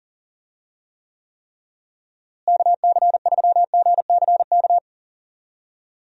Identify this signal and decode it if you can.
Morse code, heard as a keyed tone, about 40 wpm, 700 Hz. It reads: KC3GCK
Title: USA Call Signs - 6 Characters - Repeated 3x - 40wpm